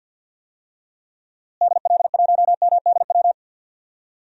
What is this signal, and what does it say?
Morse code, heard as a keyed tone, about 46 wpm, 700 Hz. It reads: DL1MDW
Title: DXCC-100 Call Signs - 6 Characters - Repeated 2x - 50wpm